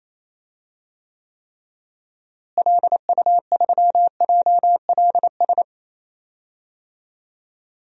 Morse code, heard as a keyed tone, about 28 wpm, 700 Hz. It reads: LU3JLS
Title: DXCC-100 Call Signs - 6 Characters - Repeated 3x - 28wpm